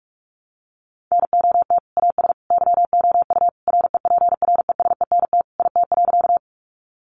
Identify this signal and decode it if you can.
Morse code, heard as a keyed tone, about 44 wpm, 700 Hz. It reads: NOT AS YOU REPRESENT IT.
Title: Sentences from Top 1200 Words 45wpm